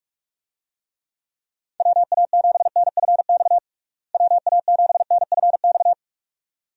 Morse code, heard as a keyed tone, about 46 wpm, 700 Hz. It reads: WA7NFX WA7NFX
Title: USA Call Signs - 6 Characters - Repeated 2x - 45wpm